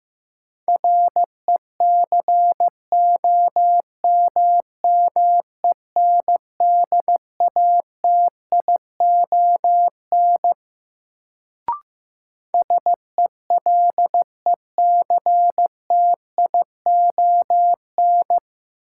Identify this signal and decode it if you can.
Morse code, heard as a keyed tone, about 15 wpm, 700 Hz. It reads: RECOMMENDATION SELECTION
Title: Sentences from Top 1400 Words - New Word Drill - 15wpm